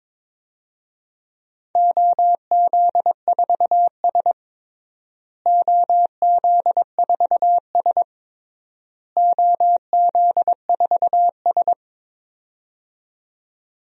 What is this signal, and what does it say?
Morse code, heard as a keyed tone, about 21 wpm, 700 Hz. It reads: OZ4S OZ4S OZ4S
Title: DXCC-100 Call Signs - 4 Characters - Repeated 3x - 22wpm